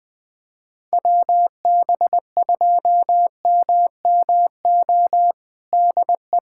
Morse code, heard as a keyed tone, about 20 wpm, 700 Hz. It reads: WB2MMO DE
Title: USA Callsign DE Callsign - 6 Characters - 20wpm